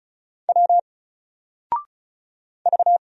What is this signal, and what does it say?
Morse code, heard as a keyed tone, about 34 wpm, 700 Hz. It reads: W V
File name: Single-Letter-Number-35wpm